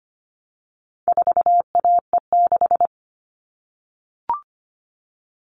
Morse code, heard as a keyed tone, about 25 wpm, 700 Hz. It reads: 4AE6
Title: All Cut Numbers - 4-Digit 25wpm